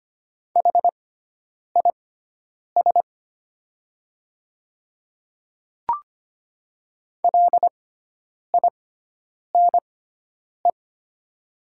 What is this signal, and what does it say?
Morse code, heard as a keyed tone, about 25 wpm, spaced as for 9 wpm, 700 Hz. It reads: HIS LINE
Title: 6x Character Spacing - Top 200 Words - 25wpm